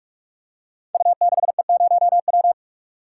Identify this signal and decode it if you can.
Morse code, heard as a keyed tone, about 44 wpm, 700 Hz. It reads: U6E0W
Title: Code Groups - 5 Letter:Numbers - 45wpm